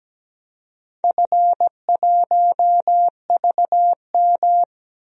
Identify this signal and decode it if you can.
Morse code, heard as a keyed tone, about 17 wpm, 700 Hz. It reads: F1VM
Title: DXCC-100 Call Signs - 4 Characters - 17wpm